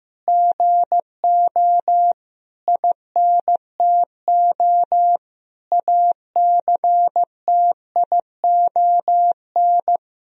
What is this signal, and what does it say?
Morse code, heard as a keyed tone, about 16 wpm, 700 Hz. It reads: GO INTO ACTION
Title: Sentences from Top 1000 Words - Review Part 4 - 15wpm